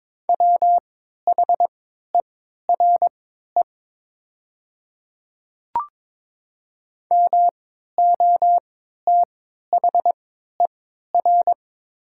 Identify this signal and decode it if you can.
Morse code, heard as a keyed tone, about 22 wpm, spaced as for 12 wpm, 700 Hz. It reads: WHERE MOTHER
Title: 3x Character Spacing - Top 200 Words - 22wpm